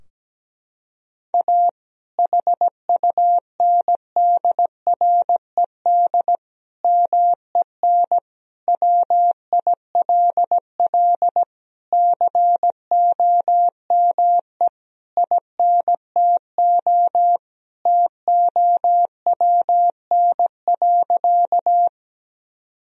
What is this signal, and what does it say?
Morse code, heard as a keyed tone, about 17 wpm, 700 Hz. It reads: A HUNDRED MEN WILL COME INTO TOWN.
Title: Sentences from Top 1000 Words - Review Part 2 - 17wpm